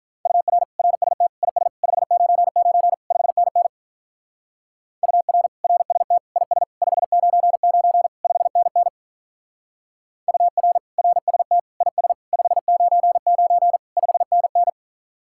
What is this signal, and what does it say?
Morse code, heard as a keyed tone, about 43 wpm, 700 Hz. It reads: UR RST IS 599 5NN UR RST IS 599 5NN UR RST IS 599 5NN
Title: Speed-Racing Common QSO Phrases - 35wpm